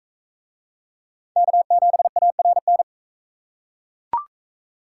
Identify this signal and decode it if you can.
Morse code, heard as a keyed tone, about 38 wpm, 700 Hz. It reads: K7ARN